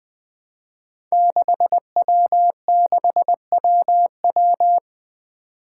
Morse code, heard as a keyed tone, about 20 wpm, 700 Hz. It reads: 6W6WW
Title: Five Challenging Letter-Number Combinations 20wpm